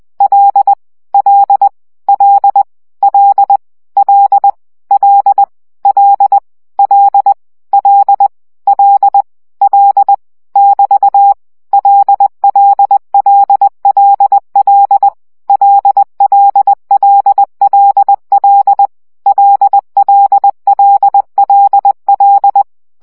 Morse code, heard as a keyed tone, about 20 wpm, 800 Hz. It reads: L L L L L L L L L L L <BT> LLLLL LLLLL LLLLL
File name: CW gyakorlas l 17 baud